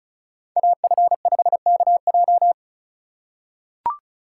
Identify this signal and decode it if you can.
Morse code, heard as a keyed tone, about 36 wpm, 700 Hz. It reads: AF5KJ
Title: USA Call Signs - 5 Characters - 2x Character Spacing - 35wpm